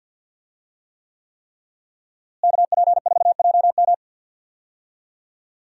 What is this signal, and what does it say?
Morse code, heard as a keyed tone, about 51 wpm, 700 Hz. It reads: KP4JM
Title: DXCC-100 Call Signs - 5 Characters - Repeated 3x - 50wpm